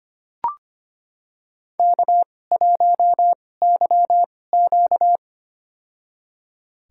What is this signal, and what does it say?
Morse code, heard as a keyed tone, about 25 wpm, spaced as for 18 wpm, 700 Hz. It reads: K1YQ
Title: USA Call Signs - 4 Characters - 2x Character Spacing - 25wpm